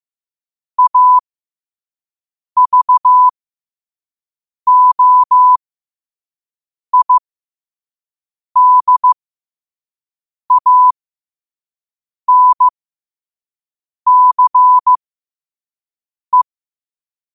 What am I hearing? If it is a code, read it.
Morse code, heard as a keyed tone, about 15 wpm, spaced as for 5 wpm, 1000 Hz. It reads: AVOIDANCE
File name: qotdfiles / 5wpm